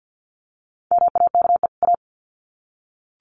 Morse code, heard as a keyed tone, about 50 wpm, 700 Hz. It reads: MAKE A
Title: Sets of 2 Words 50wpm